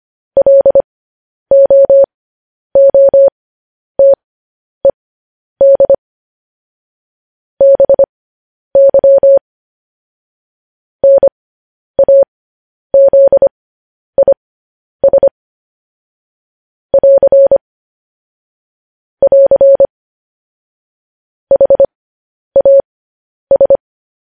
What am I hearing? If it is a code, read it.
Morse code, heard as a keyed tone, about 25 wpm, spaced as for 10 wpm, 550 Hz. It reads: LOOTED BY NAZIS <AR> <AR> HAS
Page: news-headlines-in-morse-code-at-10-wpm